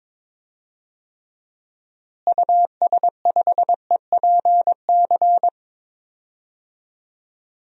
Morse code, heard as a keyed tone, about 22 wpm, 700 Hz. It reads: US5EPC
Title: DXCC-100 Call Signs - 6 Characters - Repeated 3x - 22wpm